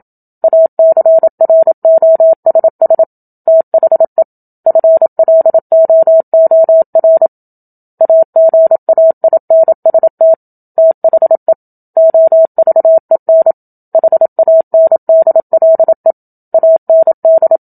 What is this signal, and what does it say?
Morse code, heard as a keyed tone, about 27 wpm, 650 Hz. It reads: ACROSS THE FLOOR AGAINST THE OVEN HANDLE AND
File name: Book Two / 01-Chapter 1 - 27 WPM